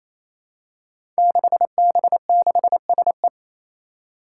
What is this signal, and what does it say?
Morse code, heard as a keyed tone, about 27 wpm, 700 Hz. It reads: 6B6SE
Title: Five Challenging Letter-Number Combinations 28wpm